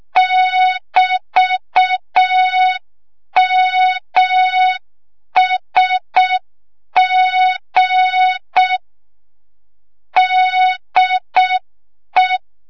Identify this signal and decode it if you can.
Morse code, heard as a keyed tone, about 6 wpm, 3700 Hz. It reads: <BT>MSG DE
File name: Morze-HA2MN / hang / 25 GYAKORLAT